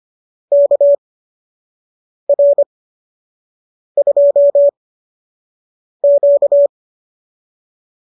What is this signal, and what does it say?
Morse code, heard as a keyed tone, about 25 wpm, spaced as for 6 wpm, 590 Hz. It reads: KR2Q